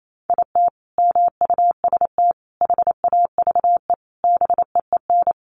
Morse code, heard as a keyed tone, about 28 wpm, 700 Hz. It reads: IT MUST HAVE BEEN
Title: Sentences from Top 800 Words 28wpm